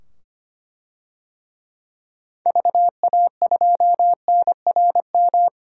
Morse code, heard as a keyed tone, about 25 wpm, 700 Hz. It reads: VA2NRM
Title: DXCC-100 Call Signs - 6 Characters - 25wpm